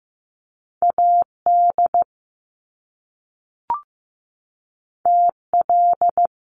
Morse code, heard as a keyed tone, about 15 wpm, 700 Hz. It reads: AD TL
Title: Top Two Letter Combinations in Words 15wpm